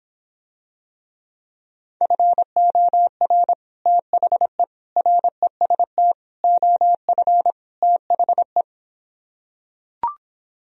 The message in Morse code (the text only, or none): FOR THE REST OF THE